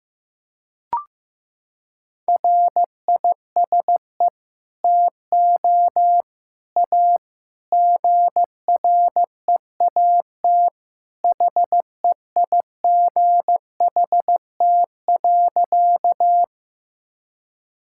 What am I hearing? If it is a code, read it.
Morse code, heard as a keyed tone, about 15 wpm, 700 Hz. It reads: RISE TO A GREAT HEIGHT.
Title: Sentences from Top 1600 Words 15wpm